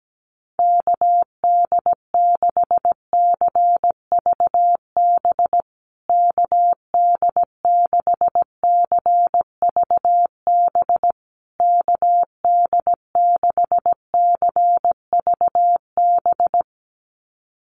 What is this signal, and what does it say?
Morse code, heard as a keyed tone, about 17 wpm, 700 Hz. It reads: KD6CVB KD6CVB KD6CVB
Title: USA Call Signs Repeated 3x 17wpm